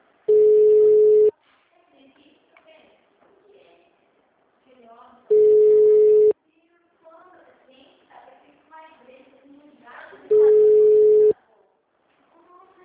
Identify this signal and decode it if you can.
Ringback tone (Europe, 425 Hz), with speech.
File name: out- -2010- - - .31